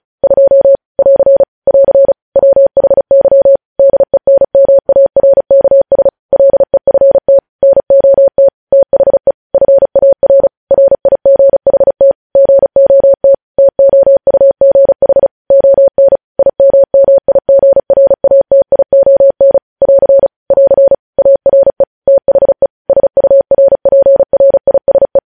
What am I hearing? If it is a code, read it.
Morse code, heard as a keyed tone, about 36 wpm, 550 Hz. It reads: 2 <AR> <AR> WHY DENMARKS LEFT NOT THE FAR RIGHT GOT TOUGH ON IMMIGRATION <AR> <AR> ARE THE SURPRISE
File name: podcast / News headlines-1x-35x35